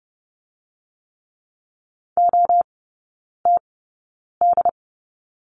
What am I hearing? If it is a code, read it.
Morse code, heard as a keyed tone, about 30 wpm, spaced as for 9 wpm, 700 Hz. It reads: OTD
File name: Lesson 16e - Letter c - Three Characters 30wpm